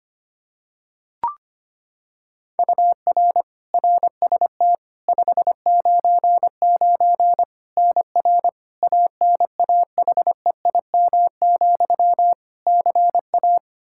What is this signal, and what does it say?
Morse code, heard as a keyed tone, about 25 wpm, 700 Hz. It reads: UR RST 599 NR ANAHEIM, CA
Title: Practice QSO - Exchange 1 - Beginning - 25wpm